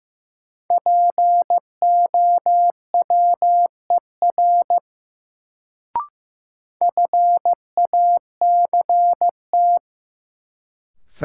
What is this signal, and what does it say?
Morse code, heard as a keyed tone, about 15 wpm, 700 Hz. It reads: POWER FACT
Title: Sentences from Top 500 Words - New Word Drill - 15wpm